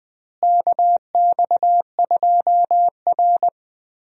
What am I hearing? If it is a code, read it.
Morse code, heard as a keyed tone, about 20 wpm, 700 Hz. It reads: KX2R
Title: USA Call Signs - 4 Characters - Repeated 3x - 20wpm